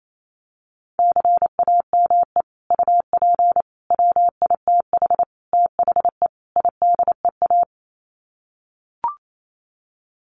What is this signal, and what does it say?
Morse code, heard as a keyed tone, about 28 wpm, 700 Hz. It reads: CAME UP WITH THE IDEA
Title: Sets of 5 Words Encore2 28wpm